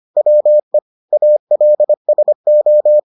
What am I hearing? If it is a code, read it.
Morse code, heard as a keyed tone, about 25 wpm, 600 Hz. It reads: WE ALSO